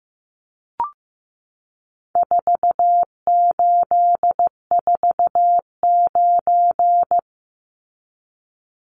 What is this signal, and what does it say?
Morse code, heard as a keyed tone, about 15 wpm, 700 Hz. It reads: 4849
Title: Numbers - 4-Digit 15wpm